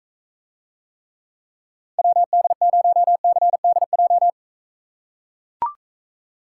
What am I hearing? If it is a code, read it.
Morse code, heard as a keyed tone, about 41 wpm, 700 Hz. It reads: WD0CDJ